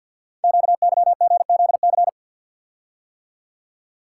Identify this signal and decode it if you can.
Morse code, heard as a keyed tone, about 51 wpm, 700 Hz. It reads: QYGZC
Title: Code Groups - 5 Letters - 50wpm